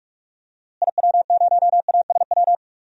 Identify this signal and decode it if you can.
Morse code, heard as a keyed tone, about 42 wpm, 700 Hz. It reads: IW0ASW
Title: DXCC-100 Call Signs - 6 Characters - Repeated 2x - 45wpm